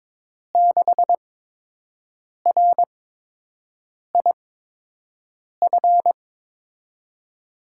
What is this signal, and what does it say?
Morse code, heard as a keyed tone, about 22 wpm, spaced as for 6 wpm, 700 Hz. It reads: 6RIF
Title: USA Call Signs - 8x Character Spacing - 22wpm